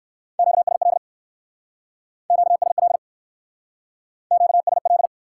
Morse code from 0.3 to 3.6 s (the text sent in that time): QSL QSL